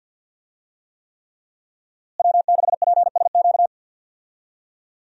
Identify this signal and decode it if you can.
Morse code, heard as a keyed tone, about 51 wpm, 700 Hz. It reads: W6PSQ
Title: USA Call Signs - 5 Characters - Repeated 3x - 50wpm